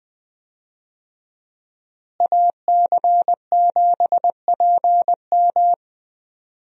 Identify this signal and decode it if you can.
Morse code, heard as a keyed tone, about 20 wpm, 700 Hz. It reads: AC7PM
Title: USA Call Signs - 5 Characters - Repeated 2x - 20wpm